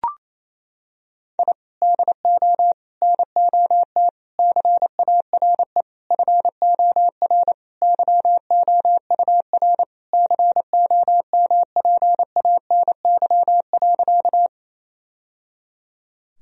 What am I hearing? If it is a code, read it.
Morse code, heard as a keyed tone, about 28 wpm, 700 Hz. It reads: I DO NOT CARE FOR YOUR COMPANY.
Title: Sentences from Top 1000 Words - Review Part 3 - 28wpm